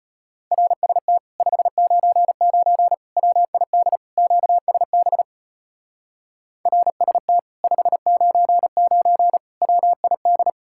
Morse code, heard as a keyed tone, about 32 wpm, 700 Hz. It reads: RST 599 WID QSB RST 599 WID